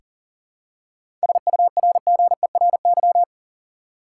Morse code, heard as a keyed tone, about 40 wpm, 700 Hz. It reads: SURGERY
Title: Seven Letter Words - 40wpm